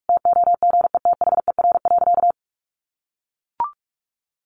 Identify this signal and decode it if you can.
Morse code, heard as a keyed tone, about 44 wpm, 700 Hz. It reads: TOGETHER.
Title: Sentences from Top 500 Words 45wpm